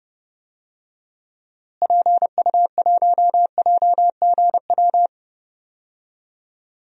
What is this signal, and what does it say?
Morse code, heard as a keyed tone, about 30 wpm, 700 Hz. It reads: PU1JGW
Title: DXCC-100 Call Signs - 6 Characters - Repeated 2x - 30wpm